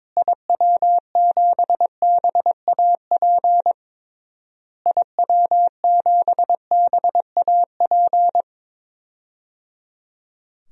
Morse code, heard as a keyed tone, about 22 wpm, 700 Hz. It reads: IW7BAP IW7BAP
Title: DXCC-100 Call Signs - 6 Characters - Repeated 3x - 22wpm